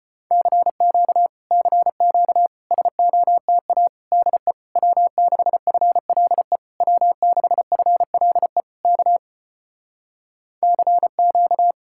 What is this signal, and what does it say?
Morse code, heard as a keyed tone, about 33 wpm, 700 Hz. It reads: CQ CQ SOTA DE W6FLE W6FLE K CQ